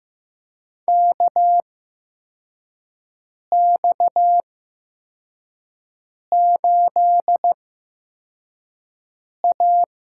Morse code, heard as a keyed tone, about 15 wpm, spaced as for 4 wpm, 700 Hz. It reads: KX8A
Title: USA Call Signs - 4 Characters - 8x Character Spacing - 15wpm